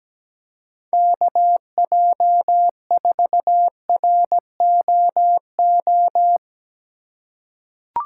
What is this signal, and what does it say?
Morse code, heard as a keyed tone, about 17 wpm, 700 Hz. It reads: KJ4ROO
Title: USA Call Signs - 6 Characters - 4x Character Spacing - 17wpm